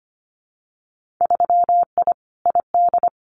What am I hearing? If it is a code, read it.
Morse code, heard as a keyed tone, about 25 wpm, 700 Hz. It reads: 3I ID
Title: Winter Field Day Exchanges 25wpm